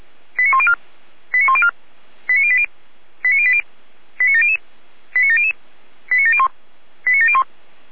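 Five-tone selective call (ZVEI), heard as quick runs of five tones, tones of about 80 ms. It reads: 89104 89104 89080 89080 89700 89700 89701 89701
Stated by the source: ZVEI five-tone selective call 89340 received via SDR.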